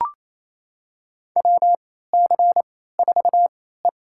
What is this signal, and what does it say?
Morse code, heard as a keyed tone, about 28 wpm, spaced as for 16 wpm, 700 Hz. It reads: WC4E
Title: USA Call Signs - 4 Characters - 3x Character Spacing - 28wpm